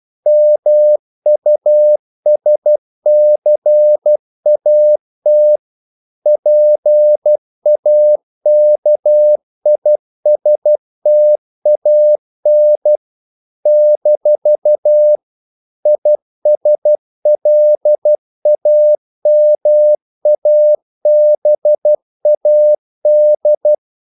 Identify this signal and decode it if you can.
Morse code, heard as a keyed tone, about 12 wpm, 600 Hz. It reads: MUSCAT PAKISTAN - ISLAMABAD
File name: CW / Practice / NationalCapitals12